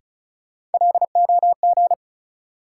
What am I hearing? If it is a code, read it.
Morse code, heard as a keyed tone, about 36 wpm, 700 Hz. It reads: LOG
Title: Sentences from Top 1400 Words - New Word Drill - 35wpm